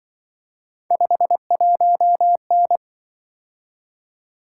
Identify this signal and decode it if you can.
Morse code, heard as a keyed tone, about 23 wpm, 700 Hz. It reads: 51N
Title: Speed-Racing - Three Letter-Number Combinations - 20wpm